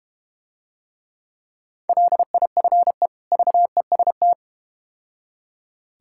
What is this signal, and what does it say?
Morse code, heard as a keyed tone, about 31 wpm, 700 Hz. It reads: LIFE VEST